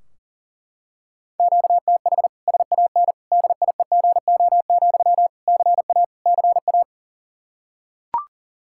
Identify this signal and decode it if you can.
Morse code, heard as a keyed tone, about 37 wpm, 700 Hz. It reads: QTH SAN DIEGO, CA CA
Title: Practice QSO - Exchange 1 - Beginning - 40wpm